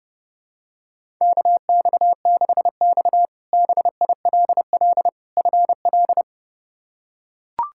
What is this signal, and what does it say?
Morse code, heard as a keyed tone, about 30 wpm, 700 Hz. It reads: KX6X BILL FL
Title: K1USN SST QSO Practice - 30wpm